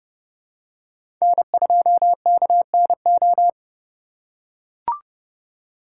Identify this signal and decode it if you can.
Morse code, heard as a keyed tone, about 29 wpm, 700 Hz. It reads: N2KNO